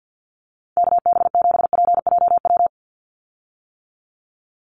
Morse code, heard as a keyed tone, about 50 wpm, 700 Hz. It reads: KB7PJW
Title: USA Call Signs 50wpm